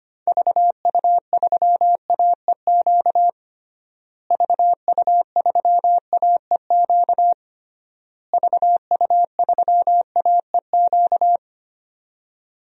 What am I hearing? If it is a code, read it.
Morse code, heard as a keyed tone, about 25 wpm, 700 Hz. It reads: VU3AEQ VU3AEQ VU3AEQ
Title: DXCC-100 Call Signs - 6 Characters - Repeated 3x - 25wpm